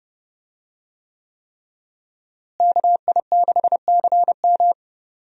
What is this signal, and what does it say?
Morse code, heard as a keyed tone, about 30 wpm, 700 Hz. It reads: KI6CM
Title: USA Call Signs - 5 Characters - Repeated 3x - 30wpm